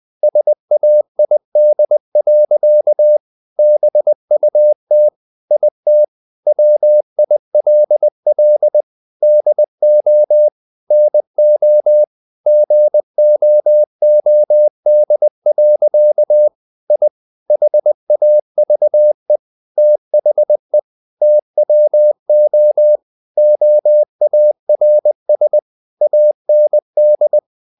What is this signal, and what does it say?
Morse code, heard as a keyed tone, about 20 wpm, 600 Hz. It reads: SAID. BUT IT WILL DO NO GOOD. I HAVE THE TWO OARS AND
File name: the old man and the sea part 2 20wpm